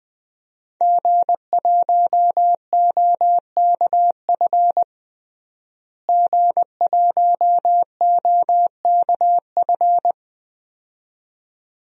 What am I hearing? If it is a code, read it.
Morse code, heard as a keyed tone, about 21 wpm, 700 Hz. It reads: G1OKF G1OKF
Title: DXCC-100 Call Signs - 5 Characters - Repeated 2x - 20wpm